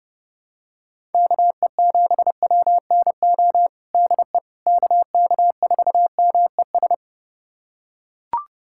Morse code, heard as a keyed tone, about 30 wpm, 700 Hz. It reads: KE7WNO DE KK4MES
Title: USA Callsign DE Callsign - 6 Characters - 30wpm